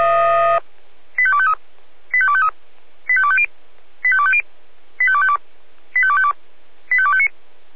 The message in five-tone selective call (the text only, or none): 85253 85253 85250 85250 85252 85252 85259